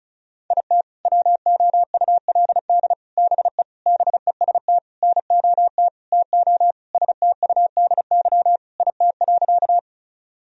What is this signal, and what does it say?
Morse code, heard as a keyed tone, about 34 wpm, 700 Hz. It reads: IT WOULD BE BEST NOT TO STUDY IT.
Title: Sentences from Top 1000 Words - Review Part 1 - 35wpm